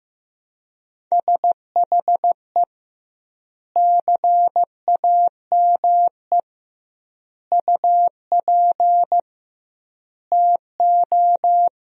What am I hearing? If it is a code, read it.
Morse code, heard as a keyed tone, about 15 wpm, 700 Hz. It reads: SHE CAME UP TO
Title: 2x Word-Spacing - Sentences from Top 200 Words - 15wpm